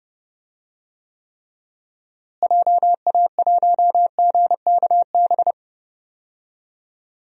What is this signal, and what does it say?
Morse code, heard as a keyed tone, about 32 wpm, 700 Hz. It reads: JA1GKB
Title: DXCC-100 Call Signs - 6 Characters - Repeated 2x - 30wpm